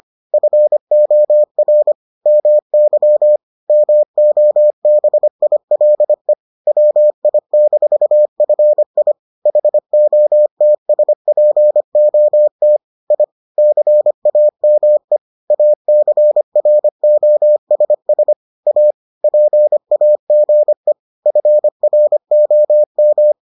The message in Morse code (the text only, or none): FOR MY MOBILE WI-FI HOTSPOT I CAME ACROSS A PAGE FROM